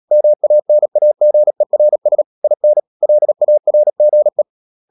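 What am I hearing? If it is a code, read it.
Morse code, heard as a keyed tone, about 37 wpm, 600 Hz. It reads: MANAGERS IN LARGE
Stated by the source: Morse code practice at 35–39 wpm sending English text of Ken Follett's novel Never, from CW med Gnister.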